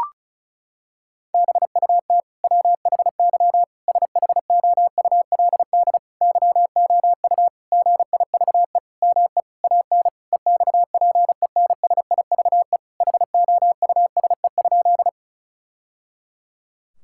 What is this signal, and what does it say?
Morse code, heard as a keyed tone, about 36 wpm, 700 Hz. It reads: BUT WHY SHOULD YOU GIVE ME AN EXPENSIVE HOUSE?
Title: Sentences from Top 1000 Words - Review Part 6 - 35wpm